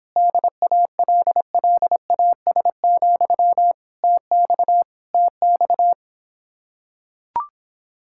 Morse code, heard as a keyed tone, about 26 wpm, 700 Hz. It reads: DALLAS, TX TX